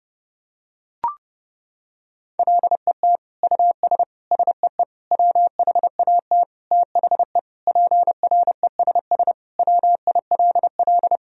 Morse code, heard as a keyed tone, about 32 wpm, 700 Hz. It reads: LET US SEE WHAT THE PRESS WILL
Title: Sentences from Top 1000 Words - Review Part 1 - 30wpm